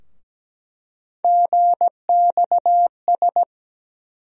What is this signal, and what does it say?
Morse code, heard as a keyed tone, about 17 wpm, 700 Hz. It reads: GXS
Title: USA Call Sign Postfixes - 3 Letters 17wpm